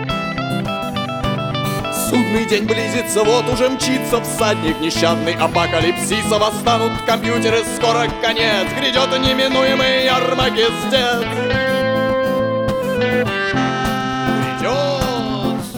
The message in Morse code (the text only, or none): none